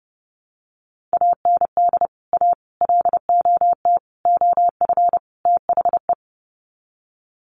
Morse code, heard as a keyed tone, about 30 wpm, 700 Hz. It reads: AND A LOT OF THE
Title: Sets of 5 Words Encore 30wpm